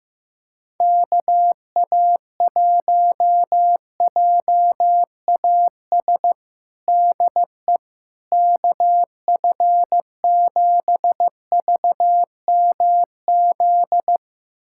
Morse code, heard as a keyed tone, about 15 wpm, 700 Hz. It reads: KA1JAS DE KF7VMZ
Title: USA Callsign DE Callsign - 6 Characters - 15wpm